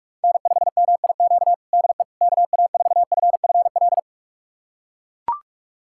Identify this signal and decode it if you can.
Morse code, heard as a keyed tone, about 41 wpm, 700 Hz. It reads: N5MIQ DE KA4FFL
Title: USA Callsign DE Callsign 45wpm